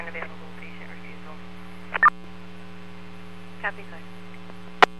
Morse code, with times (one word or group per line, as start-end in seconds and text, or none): none